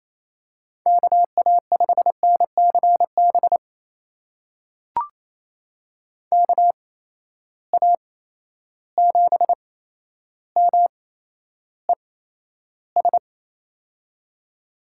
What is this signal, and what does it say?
Morse code, heard as a keyed tone, about 28 wpm, spaced as for 12 wpm, 700 Hz. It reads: KA5NCB KA7MES
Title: USA Call Signs - 6 Characters - 8x Character Spacing - 28wpm